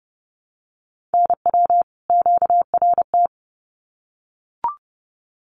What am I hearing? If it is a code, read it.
Morse code, heard as a keyed tone, about 29 wpm, 700 Hz. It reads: NW QRT
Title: Common QSO Phrases 30wpm